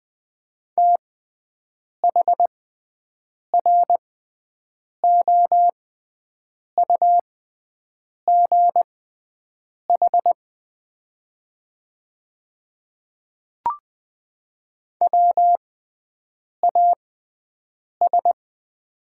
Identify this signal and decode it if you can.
Morse code, heard as a keyed tone, about 20 wpm, spaced as for 7 wpm, 700 Hz. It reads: THROUGH WAS
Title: 6x Character Spacing - Top 200 Words - 20wpm